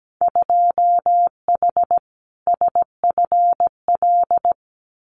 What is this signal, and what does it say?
Morse code, heard as a keyed tone, about 17 wpm, 700 Hz. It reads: 2H SFL
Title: Winter Field Day Exchanges Encore 17wpm